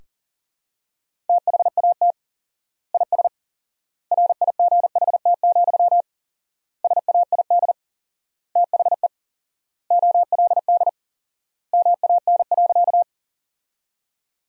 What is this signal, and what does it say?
Morse code, heard as a keyed tone, about 40 wpm, 700 Hz. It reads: THAT IS RIGHT, SAID THE OLD MAN.
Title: 4x Word-Spacing - Sentences from Top 200 Words - 40wpm